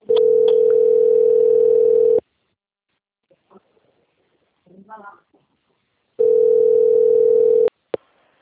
Ringback tone (North America), 440 + 480 Hz.
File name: out- - - - - .553